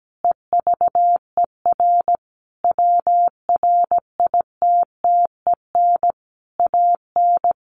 Morse code, heard as a keyed tone, about 17 wpm, 700 Hz. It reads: EVER WRITTEN AN
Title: Sentences from Top 1500 Words 17wpm